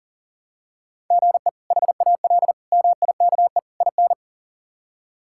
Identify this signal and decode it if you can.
Morse code, heard as a keyed tone, about 40 wpm, 700 Hz. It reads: GE HAL MIKE IN
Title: K1USN SST QSO Practice - 40wpm